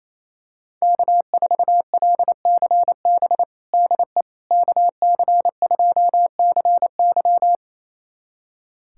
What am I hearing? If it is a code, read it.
Morse code, heard as a keyed tone, about 28 wpm, 700 Hz. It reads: K4LCB DE KC2CY
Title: USA Callsign DE Callsign - 5 Characters - 28wpm